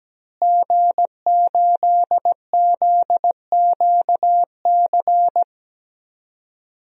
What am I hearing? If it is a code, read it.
Morse code, heard as a keyed tone, about 17 wpm, 700 Hz. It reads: G8ZQC
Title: DXCC-100 Call Signs - 5 Characters - Repeated 3x - 17wpm